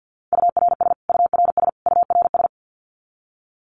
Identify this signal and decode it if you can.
Morse code, heard as a keyed tone, about 50 wpm, 700 Hz. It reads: URS URS URS
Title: Abbreviations Repeated 3x - 50wpm